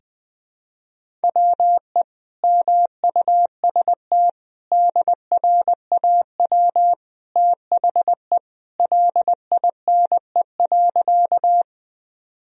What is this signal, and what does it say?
Morse code, heard as a keyed tone, about 20 wpm, 700 Hz. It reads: WE MUST DRAW THE LINE.
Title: Sentences from Top 1000 Words - Review Part 1 - 20wpm